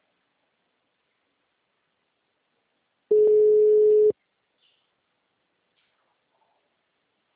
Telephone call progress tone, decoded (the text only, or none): ringback tone